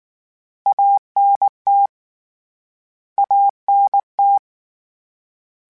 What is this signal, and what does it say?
Morse code, heard as a keyed tone, about 19 wpm, 800 Hz. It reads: ANT ANT
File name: Abreviations / Abreviations mp3s / Abreviations@18